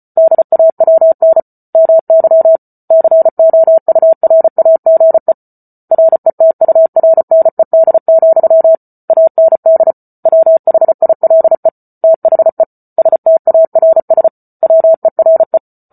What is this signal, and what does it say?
Morse code, heard as a keyed tone, about 34 wpm, 650 Hz. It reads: DAWN MY COURAGE RETURNED, AND WHILE THE STARS WERE